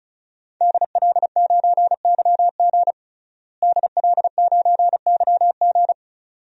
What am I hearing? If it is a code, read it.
Morse code, heard as a keyed tone, about 36 wpm, 700 Hz. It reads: DL9YG DL9YG
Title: DXCC-100 Call Signs - 5 Characters - Repeated 2x - 35wpm